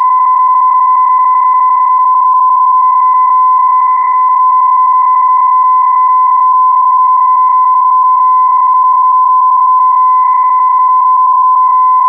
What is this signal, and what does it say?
PSK31, about 31.25 baud.